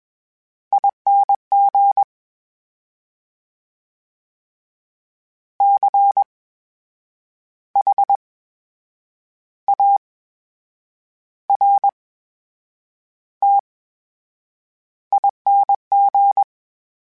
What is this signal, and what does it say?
Morse code, heard as a keyed tone, about 21 wpm, spaced as for 8 wpm, 800 Hz. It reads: ING CHARTING
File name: Special INGs-mp3s / Special INGs-mp3s / VerySlowToFaster Ing @20